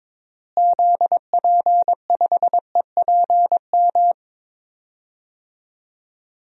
Morse code, heard as a keyed tone, about 21 wpm, 700 Hz. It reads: ZP5EPM
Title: DXCC-100 Call Signs - 6 Characters - Repeated 2x - 22wpm